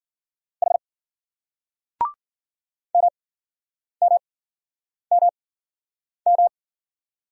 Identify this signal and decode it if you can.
Morse code, heard as a keyed tone, about 54 wpm, spaced as for 9 wpm, 700 Hz. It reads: H MMMM